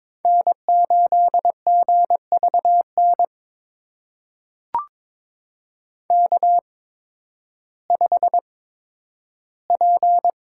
Morse code, heard as a keyed tone, about 22 wpm, spaced as for 12 wpm, 700 Hz. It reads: N8GVN K5P
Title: USA Call Signs - 5 Characters - 8x Character Spacing - 22wpm